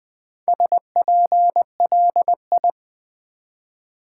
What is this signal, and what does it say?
Morse code, heard as a keyed tone, about 20 wpm, 700 Hz. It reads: SPLI
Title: Build A Word - 1 Letter At A Time - 5-Letter Words - 20wpm